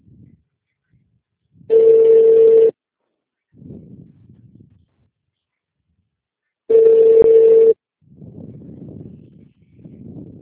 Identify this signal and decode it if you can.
Ringback tone (China, 450 Hz).
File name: out- -2011- - - .20191